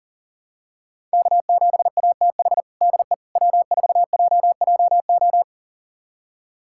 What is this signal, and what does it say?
Morse code, heard as a keyed tone, about 40 wpm, 700 Hz. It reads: K7ATH DE W4JJO
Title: USA Callsign DE Callsign - 5 Characters - 40wpm